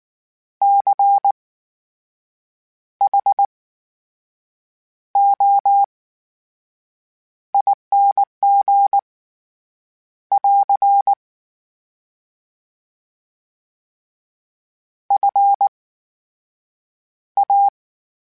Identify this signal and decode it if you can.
Morse code, heard as a keyed tone, about 19 wpm, spaced as for 6 wpm, 800 Hz. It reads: CHOING <AR> FA
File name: Special INGs / Special INGs-mp3s / VerySlowToFaster Ing @18